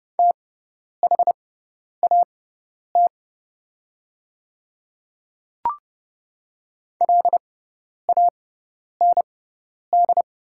Morse code, heard as a keyed tone, about 30 wpm, spaced as for 10 wpm, 700 Hz. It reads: THAT LAND
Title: 6x Character Spacing - Top 200 Words - 30wpm